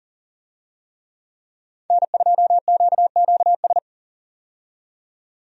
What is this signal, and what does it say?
Morse code, heard as a keyed tone, about 40 wpm, 700 Hz. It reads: N2QQS
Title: USA Call Signs - 5 Characters - 40wpm